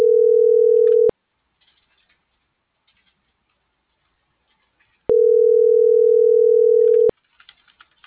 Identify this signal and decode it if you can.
Ringback tone (North America), 440 + 480 Hz.